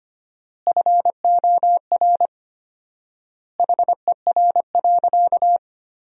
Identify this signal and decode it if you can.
Morse code, heard as a keyed tone, about 25 wpm, 700 Hz. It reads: FOR HER.
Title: 4x Word-Spacing - Sentences from Top 400 Words - 25wpm